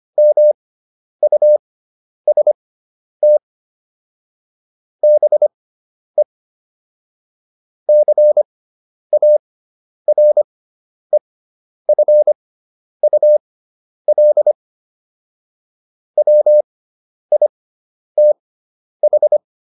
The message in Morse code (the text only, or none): MUST BE CAREFUL WITH